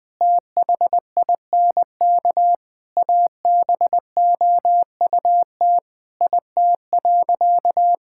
Morse code, heard as a keyed tone, about 20 wpm, 700 Hz. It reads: THINK ABOUT IT.
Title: Sentences with 7-Word Phrases - 20wpm